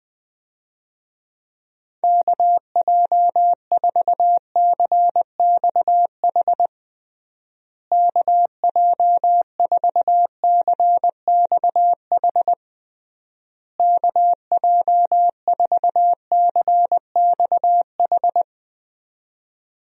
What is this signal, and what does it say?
Morse code, heard as a keyed tone, about 21 wpm, 700 Hz. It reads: KJ4CXH KJ4CXH KJ4CXH
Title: USA Call Signs - 6 Characters - Repeated 3x - 20wpm